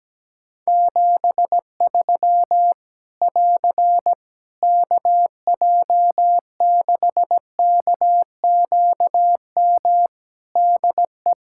Morse code, heard as a keyed tone, about 17 wpm, 700 Hz. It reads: 73 <AR> KJ6KQM DE
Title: Practice QSO - Exchange 4 - Ending - 17wpm